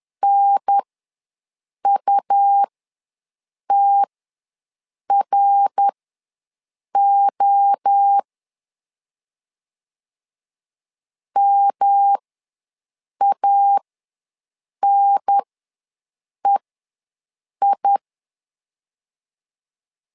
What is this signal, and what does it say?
Morse code, heard as a keyed tone, about 10 wpm, spaced as for 6 wpm, 800 Hz. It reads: NUTRO MANEI
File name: LEZIONI / 3 MWRO / 3 MWRO ws 10wpm 2310